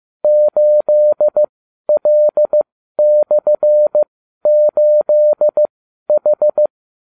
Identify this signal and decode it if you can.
Morse code, heard as a keyed tone, about 15 wpm, spaced as for 12 wpm, 600 Hz. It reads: 8L/8H